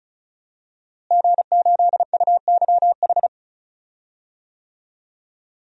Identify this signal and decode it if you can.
Morse code, heard as a keyed tone, about 36 wpm, 700 Hz. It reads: G8UYH
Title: DXCC-100 Call Signs - 5 Characters - Repeated 3x - 35wpm